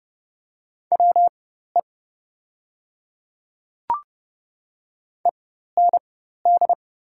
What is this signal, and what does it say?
Morse code, heard as a keyed tone, about 30 wpm, spaced as for 14 wpm, 700 Hz. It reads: WE END
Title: 4x Character Spacing - Top 200 Words - 30wpm